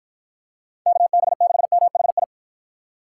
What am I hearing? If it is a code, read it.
Morse code, heard as a keyed tone, about 52 wpm, 700 Hz. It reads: KB6MHI